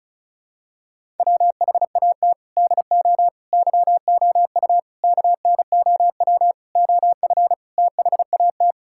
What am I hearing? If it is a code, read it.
Morse code, heard as a keyed tone, about 34 wpm, 700 Hz. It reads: WHAT DO YOU KNOW OF THAT
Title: Sentences from Top 1000 Words - Review Part 5 - 35wpm